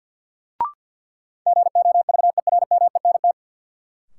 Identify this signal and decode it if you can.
Morse code, heard as a keyed tone, about 52 wpm, 700 Hz. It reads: GOVERMENT
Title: Sentences from Top 1300 Words - New Word Drill Alt - 50wpm